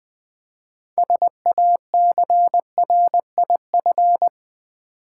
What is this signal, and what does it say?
Morse code, heard as a keyed tone, about 20 wpm, 700 Hz. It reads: SACRIF
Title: Build A Word - 3 Letters At A Time - 9-Letter Words - 20wpm